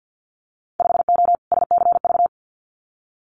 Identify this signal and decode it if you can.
Morse code, heard as a keyed tone, about 50 wpm, 700 Hz. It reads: 5O SCV
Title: Winter Field Day Exchanges Encore 50wpm